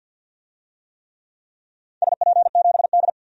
Morse code, heard as a keyed tone, about 49 wpm, 700 Hz. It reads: SP7D
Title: DXCC-100 Call Signs - 4 Characters - 50wpm